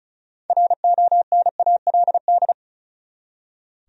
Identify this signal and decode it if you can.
Morse code, heard as a keyed tone, about 34 wpm, 700 Hz. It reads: RONALD
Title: USA Common Names - 35wpm